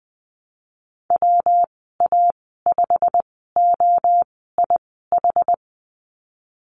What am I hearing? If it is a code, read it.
Morse code, heard as a keyed tone, about 19 wpm, spaced as for 14 wpm, 700 Hz. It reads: WA5OIH
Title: 128 Lesson 21g - Number 6 - Call Signs 20wpm